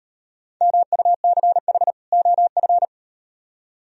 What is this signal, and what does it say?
Morse code, heard as a keyed tone, about 36 wpm, 700 Hz. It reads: MUCH OF